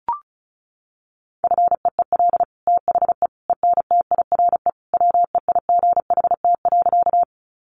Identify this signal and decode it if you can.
Morse code, heard as a keyed tone, about 34 wpm, 700 Hz. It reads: FEEL THE ENTIRE WEIGHT.
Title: Sentences from Top 800 Words 35wpm